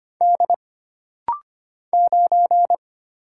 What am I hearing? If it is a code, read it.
Morse code, heard as a keyed tone, about 25 wpm, 700 Hz. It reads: D 9
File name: Single Character (Letter:Number:Punctuation:Prosign) - Rapid-Fire - 25wpm